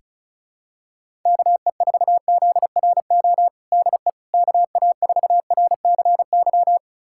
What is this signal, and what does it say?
Morse code, heard as a keyed tone, about 34 wpm, 700 Hz. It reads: KE4ZRO DE KA4RCY
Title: USA Callsign DE Callsign 35wpm